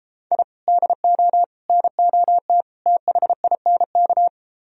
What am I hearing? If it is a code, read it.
Morse code, heard as a keyed tone, about 33 wpm, 700 Hz. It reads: I DO NOT THINK